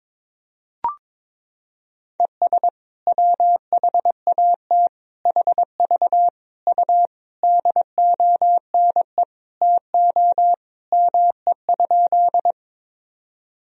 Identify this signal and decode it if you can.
Morse code, heard as a keyed tone, about 22 wpm, 700 Hz. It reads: ES WHAT HV U DONE TO ME?
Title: Sentences from Top 500 Words and Abbreviations 22wpm